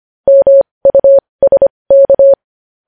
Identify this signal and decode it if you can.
Morse code, heard as a keyed tone, about 25 wpm, spaced as for 20 wpm, 550 Hz. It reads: MUSK
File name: podcast / News headlines-1x-25x20